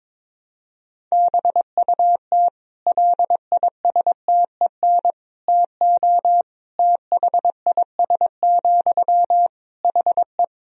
Morse code, heard as a keyed tone, about 22 wpm, 700 Hz. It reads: BUT LISTEN TO THIS, HE
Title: Sentences from Top 400 Words and Abbreviations 22wpm